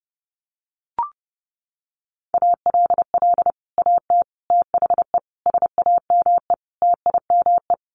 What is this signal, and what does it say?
Morse code, heard as a keyed tone, about 28 wpm, 700 Hz. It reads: ALL AT THE SAME TIME
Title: Sets of 5 Words Encore2 30wpm